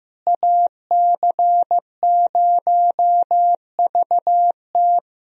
Morse code, heard as a keyed tone, about 15 wpm, 700 Hz. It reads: AC0VT
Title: USA Call Signs - 5 Characters - Repeated 2x - 15wpm